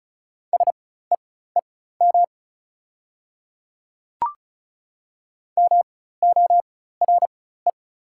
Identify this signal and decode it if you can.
Morse code, heard as a keyed tone, about 36 wpm, spaced as for 16 wpm, 700 Hz. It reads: SEEM MORE
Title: 4x Character Spacing - Top 300 Words - 35wpm